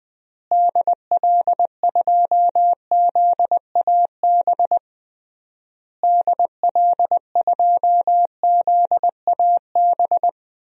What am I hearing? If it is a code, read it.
Morse code, heard as a keyed tone, about 20 wpm, 700 Hz. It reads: DL2ZAB DL2ZAB
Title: DXCC-100 Call Signs - 6 Characters - Repeated 3x - 20wpm